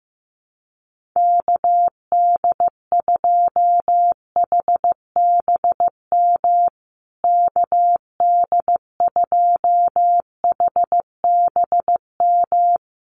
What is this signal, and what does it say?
Morse code, heard as a keyed tone, about 15 wpm, 700 Hz. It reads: KD2HBM KD2HBM
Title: USA Call Signs Repeated 3x 15wpm